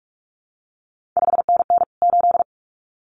Morse code, heard as a keyed tone, about 46 wpm, 700 Hz. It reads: 5NN 8
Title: CQ WW DX Contest Exchange - 45wpm